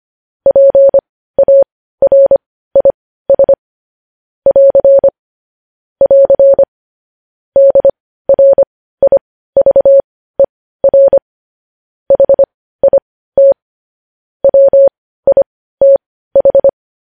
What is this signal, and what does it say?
Morse code, heard as a keyed tone, about 25 wpm, spaced as for 15 wpm, 550 Hz. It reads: PARIS <AR> <AR> DRIVER HIT WITH
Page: news-headlines-in-morse-code-at-15-wpm